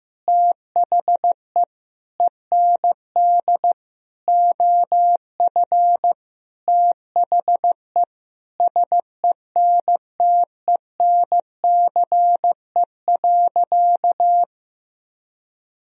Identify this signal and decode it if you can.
Morse code, heard as a keyed tone, about 15 wpm, 700 Hz. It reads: THE END OF THE SENTENCE.
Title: Sentences from Top 1000 Words - Review Part 1 - 15wpm